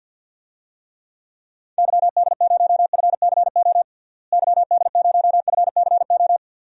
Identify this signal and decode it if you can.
Morse code, heard as a keyed tone, about 50 wpm, 700 Hz. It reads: YD0FCO YD0FCO
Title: DXCC-100 Call Signs - 6 Characters - Repeated 2x - 50wpm